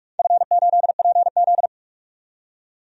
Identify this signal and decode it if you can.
Morse code, heard as a keyed tone, about 46 wpm, 700 Hz. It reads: F8PZ
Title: DXCC-100 Call Signs - 4 Characters - 45wpm